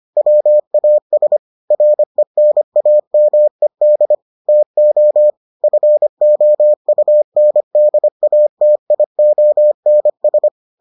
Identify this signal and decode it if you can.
Morse code, heard as a keyed tone, about 25 wpm, 600 Hz. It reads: WAS RENAMED TO FOUNDATIONS